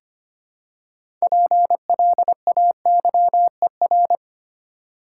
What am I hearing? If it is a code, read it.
Morse code, heard as a keyed tone, about 25 wpm, 700 Hz. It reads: PLAYER
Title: Six Letter Words - 25wpm